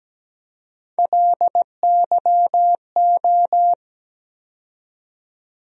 Morse code, heard as a keyed tone, about 17 wpm, 700 Hz. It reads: LYO
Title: USA Call Sign Postfixes - 3 Letters 17wpm